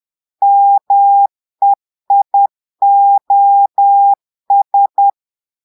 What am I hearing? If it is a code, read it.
Morse code, heard as a keyed tone, about 10 wpm, 800 Hz. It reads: MEIOS
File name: portugais texte en clair 2 vitesse 10 mots minutes